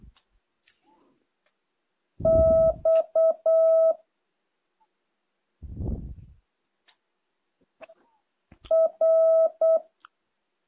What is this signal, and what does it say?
Morse code, heard as a keyed tone, about 8 wpm, 650 Hz. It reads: X R
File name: jh8cbh / 2023cw